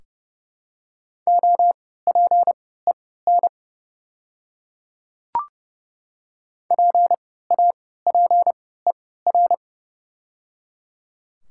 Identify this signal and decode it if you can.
Morse code, heard as a keyed tone, about 30 wpm, spaced as for 17 wpm, 700 Hz. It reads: OPEN PAPER
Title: 3x Character Spacing - Top 500 Words - 30wpm